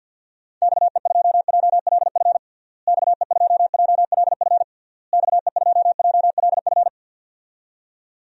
Morse code, heard as a keyed tone, about 52 wpm, 700 Hz. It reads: XE2JLF XE2JLF XE2JLF
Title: DXCC-100 Call Signs - 6 Characters - Repeated 3x - 50wpm